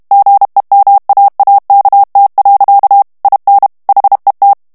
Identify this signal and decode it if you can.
Morse code, heard as a keyed tone, about 32 wpm, 800 Hz. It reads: GEMAAKT. IN HET
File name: morse mp3 / HOLLANDAIS texte clair 1 vitesse 30 mots minute